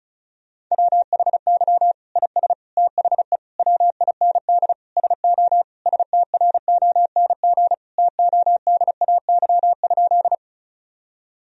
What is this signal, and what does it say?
Morse code, heard as a keyed tone, about 34 wpm, 700 Hz. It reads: WHY IS THE WIND SO STRONG TODAY?
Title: Sentences from Top 1000 Words - Review Part 2 - 35wpm